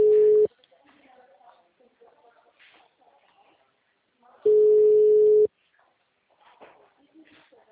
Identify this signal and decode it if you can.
Ringback tone (Europe, 425 Hz), with speech.
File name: out- -2014- - - .476